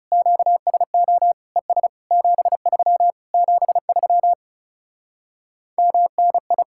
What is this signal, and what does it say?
Morse code, heard as a keyed tone, about 40 wpm, 700 Hz. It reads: QSO ES 73 73 MNI